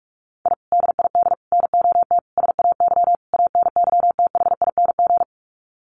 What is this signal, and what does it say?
Morse code, heard as a keyed tone, about 44 wpm, 700 Hz. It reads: I DID NOT SAY ANYTHING
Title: Sets of 5 Words Encore2 45wpm